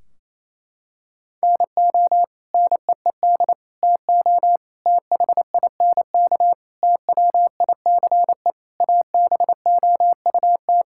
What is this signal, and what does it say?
Morse code, heard as a keyed tone, about 27 wpm, 700 Hz. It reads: NO NEED TO THINK TWICE ABOUT
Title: Sentences from Top 1600 Words 28wpm